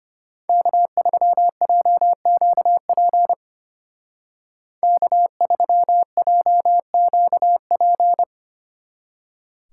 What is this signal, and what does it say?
Morse code, heard as a keyed tone, about 30 wpm, 700 Hz. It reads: K3JQP K3JQP
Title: Speed Racing - USA 1x3 Call Signs 25wpm